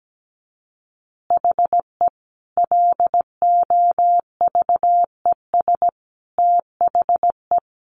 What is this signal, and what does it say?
Morse code, heard as a keyed tone, about 17 wpm, 700 Hz. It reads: HE LOVES THE
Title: Additional Easy Phrases 17wpm